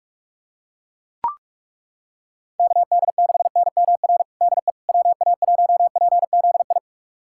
Morse code, heard as a keyed tone, about 41 wpm, 700 Hz. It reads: KD6NMR DE WA1PZI
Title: USA Callsign DE Callsign 45wpm